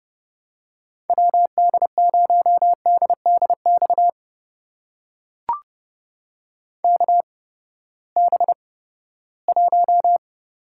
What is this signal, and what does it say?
Morse code, heard as a keyed tone, about 30 wpm, spaced as for 17 wpm, 700 Hz. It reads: WD0DDX KB1
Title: USA Call Signs - 6 Characters - 8x Character Spacing - 30wpm